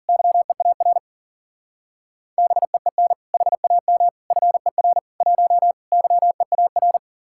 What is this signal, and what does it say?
Morse code, heard as a keyed tone, about 49 wpm, 700 Hz. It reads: YEAR BEEN HAM FER 1 YEAR
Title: Speed-Racing Common QSO Phrases Encore - 40wpm